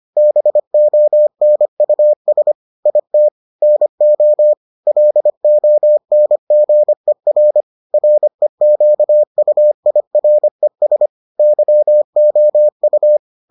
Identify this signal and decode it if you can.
Morse code, heard as a keyed tone, about 25 wpm, 600 Hz. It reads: BONUS IT NO LONGER REQUIRES YOU